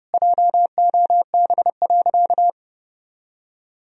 Morse code, heard as a keyed tone, about 32 wpm, 700 Hz. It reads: JOB.
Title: Sentences from Top 1000 Words - Review Part 3 - 30wpm